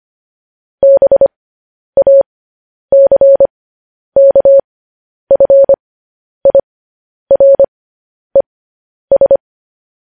Morse code, heard as a keyed tone, about 25 wpm, spaced as for 10 wpm, 550 Hz. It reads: BACKFIRES
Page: News Headlines in Morse Code at 10 WPM